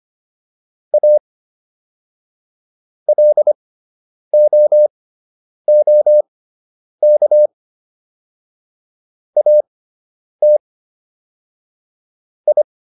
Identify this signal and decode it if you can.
Morse code, heard as a keyed tone, about 25 wpm, spaced as for 9 wpm, 600 Hz. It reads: A LOOK AT I